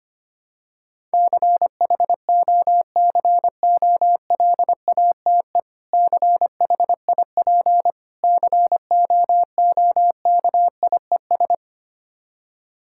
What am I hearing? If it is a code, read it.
Morse code, heard as a keyed tone, about 25 wpm, 700 Hz. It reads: CHOCOLATE CHIP COOKIES
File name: practice / 25wpm / Difficult Words with Hints - 25wpm